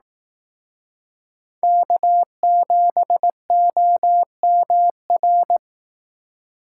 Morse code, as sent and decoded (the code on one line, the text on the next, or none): -.- --... --- -- .-.
K7OMR